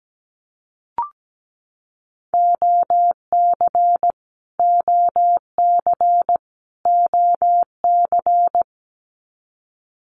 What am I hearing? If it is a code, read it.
Morse code, heard as a keyed tone, about 17 wpm, 700 Hz. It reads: OC OC OC
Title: Abbreviations Repeated 3x - 17wpm